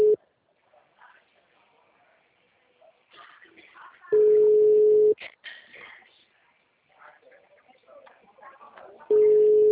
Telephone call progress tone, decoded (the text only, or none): ringback tone